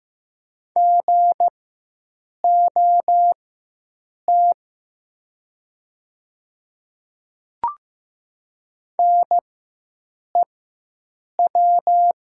Morse code, heard as a keyed tone, about 15 wpm, spaced as for 7 wpm, 700 Hz. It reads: GOT NEW
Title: 4x Character Spacing - Top 300 Words - 15wpm